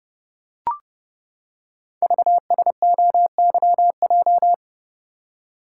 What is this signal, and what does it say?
Morse code, heard as a keyed tone, about 28 wpm, 690 Hz. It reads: VSOYJ
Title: Code Groups - 5 Letters - 30wpm